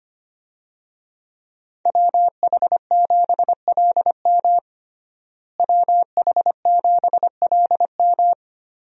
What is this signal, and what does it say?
Morse code, heard as a keyed tone, about 25 wpm, 700 Hz. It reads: WH7LM WH7LM
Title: USA Call Signs - 5 Characters - Repeated 3x - 25wpm